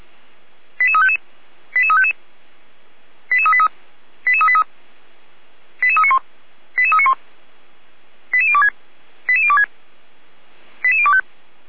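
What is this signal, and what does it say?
Five-tone selective call (ZVEI), heard as quick runs of five tones, tones of about 70 ms. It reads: 88355 88355 88383 88383 88391 88391 88026 88026 88025
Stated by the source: ZVEI five-tone selective call received on SDR.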